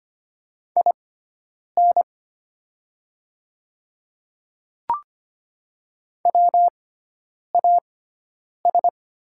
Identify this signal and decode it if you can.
Morse code, heard as a keyed tone, about 25 wpm, spaced as for 9 wpm, 700 Hz. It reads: IN WAS
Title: 6x Character Spacing - Top 100 Words - 25wpm